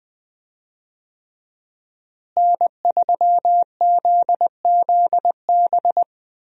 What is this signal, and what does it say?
Morse code, heard as a keyed tone, about 19 wpm, 700 Hz. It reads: N3ZZB
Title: USA Call Signs - 5 Characters - Repeated 3x - 20wpm